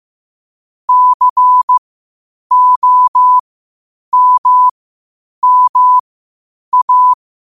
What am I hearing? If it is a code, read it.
Morse code, heard as a keyed tone, about 15 wpm, spaced as for 8 wpm, 1000 Hz. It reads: COMMA
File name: qotdfiles / 8wpm